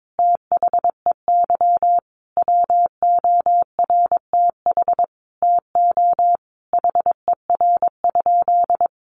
Morse code, heard as a keyed tone, about 22 wpm, 700 Hz. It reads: THEY WORTH TO HER?
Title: Sentences from Top 800 Words 22wpm